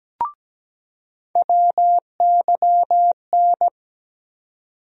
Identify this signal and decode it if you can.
Morse code, heard as a keyed tone, about 17 wpm, 700 Hz. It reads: WYN
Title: USA Call Sign Postfixes - 3 Letters 17wpm